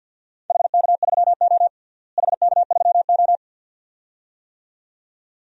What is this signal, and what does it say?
Morse code, heard as a keyed tone, about 51 wpm, 700 Hz. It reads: HK3O HK3O
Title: DXCC-100 Call Signs - 4 Characters - Repeated 2x - 50wpm